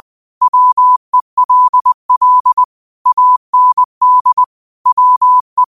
Morse code, heard as a keyed tone, about 20 wpm, 1000 Hz. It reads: WELL AND WE